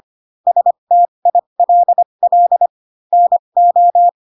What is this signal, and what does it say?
Morse code, heard as a keyed tone, about 25 wpm, spaced as for 22 wpm, 700 Hz. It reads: STILL NO